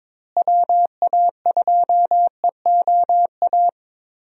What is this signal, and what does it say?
Morse code, heard as a keyed tone, about 22 wpm, 700 Hz. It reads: WA2EOA
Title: USA Call Signs - 6 Characters - Repeated 3x - 22wpm